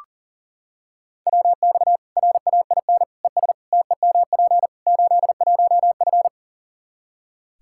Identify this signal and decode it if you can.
Morse code, heard as a keyed tone, about 37 wpm, 700 Hz. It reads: WX RAIN ES TEMP 81F
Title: Practice QSO - Exchange 2 - Info - 40wpm